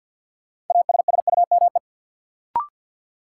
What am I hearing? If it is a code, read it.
Morse code, heard as a keyed tone, about 53 wpm, 700 Hz. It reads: ASSUME
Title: Sentences from Top 1500 Words - New Word Drill - 50wpm